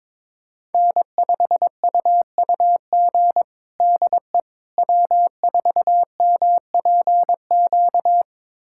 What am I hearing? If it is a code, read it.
Morse code, heard as a keyed tone, about 22 wpm, 700 Hz. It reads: N5UUG DE W4MPQ
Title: USA Callsign DE Callsign - 5 Characters - 22wpm